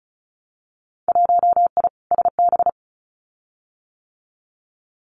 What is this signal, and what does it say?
Morse code, heard as a keyed tone, about 34 wpm, 700 Hz. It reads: 1I SB
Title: Winter Field Day Exchanges 35wpm